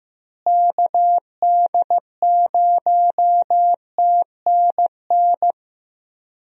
Morse code, heard as a keyed tone, about 15 wpm, 700 Hz. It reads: KD0TNN
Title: USA Call Signs - 6 Characters - Repeated 3x - 15wpm